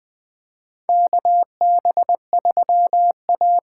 Morse code, heard as a keyed tone, about 20 wpm, 700 Hz. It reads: KB3A
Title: USA Call Signs - 6 Characters - 8x Character Spacing - 20wpm